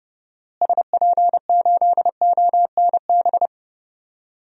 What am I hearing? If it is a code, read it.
Morse code, heard as a keyed tone, about 30 wpm, 700 Hz. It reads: SP8ONB
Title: DXCC-100 Call Signs - 6 Characters - Repeated 3x - 30wpm